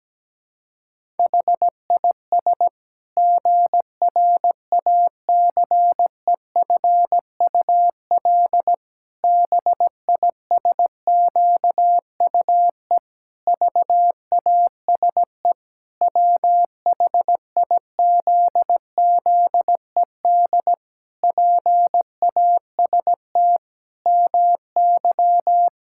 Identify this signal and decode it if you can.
Morse code, heard as a keyed tone, about 17 wpm, 700 Hz. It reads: HIS GRACEFUL BISQUE VASE WHIZZED PAST MY
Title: Pangram Challenge v2 - 17wpm